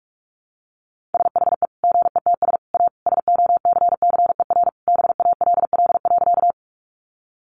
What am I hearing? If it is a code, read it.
Morse code, heard as a keyed tone, about 44 wpm, 700 Hz. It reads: SHE GETS A SOCCER BALL.
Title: Easy Phrases 45wpm